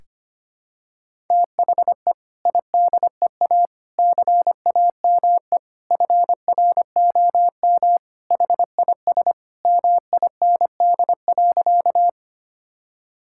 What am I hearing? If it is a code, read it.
Morse code, heard as a keyed tone, about 25 wpm, 700 Hz. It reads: THE IDEA CAME FROM HIS MIND.
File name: practice / 25wpm / Sentences with 2-Word Phrases - 25wpm